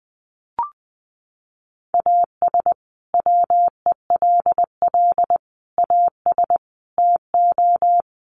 Morse code, heard as a keyed tone, about 20 wpm, 700 Hz. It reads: AS WELL AS TO
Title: Sets of 5 Words Encore2 20wpm